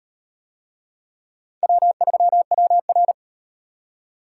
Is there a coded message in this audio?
Yes: Morse code from 1.6 to 3.1 s, heard as a keyed tone, about 38 wpm, 700 Hz. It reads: W3WR